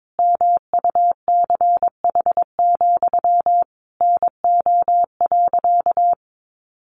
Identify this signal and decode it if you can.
Morse code, heard as a keyed tone, about 22 wpm, 700 Hz. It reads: MUCH, NO.
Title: A Sound Of Thunder 22wpm